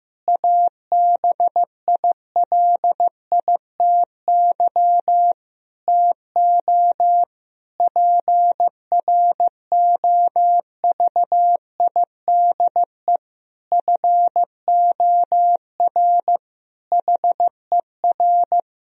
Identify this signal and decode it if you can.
Morse code, heard as a keyed tone, about 16 wpm, 700 Hz. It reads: ABILITY TO PROVIDE FOR HER
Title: Sentences from Top 1000 Words - Review Part 3 - 15wpm